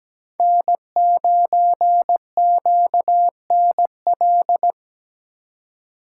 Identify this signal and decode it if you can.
Morse code, heard as a keyed tone, about 20 wpm, 700 Hz. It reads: N9QNL
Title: Speed Racing - USA Call Signs 17wpm